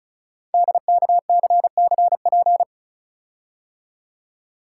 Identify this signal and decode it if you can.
Morse code, heard as a keyed tone, about 36 wpm, 700 Hz. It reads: DKCCP
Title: Code Groups - 5 Letters - 35wpm